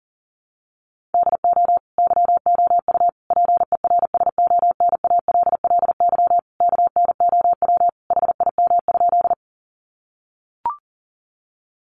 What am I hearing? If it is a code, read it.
Morse code, heard as a keyed tone, about 40 wpm, 700 Hz. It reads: DO YOU PERSONALLY KNOW HIM?
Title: Sentences from Top 1400 Words 40wpm